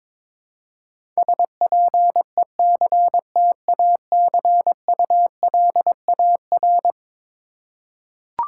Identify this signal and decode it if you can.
Morse code, heard as a keyed tone, about 21 wpm, 700 Hz. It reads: SPECTACULAR
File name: practice / 22wpm / Eleven Letter Words - 22wpm